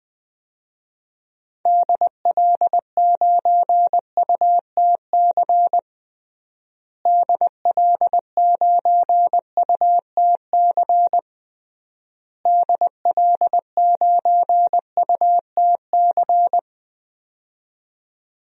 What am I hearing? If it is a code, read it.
Morse code, heard as a keyed tone, about 20 wpm, 700 Hz. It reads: DL9UTC DL9UTC DL9UTC
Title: DXCC-100 Call Signs - 6 Characters - Repeated 3x - 20wpm